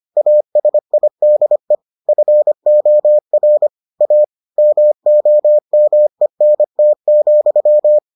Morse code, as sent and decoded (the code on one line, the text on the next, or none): .- ... .. -.. / ..-. --- .-. / .- / -- --- -- . -. - --..--
ASIDE FOR A MOMENT,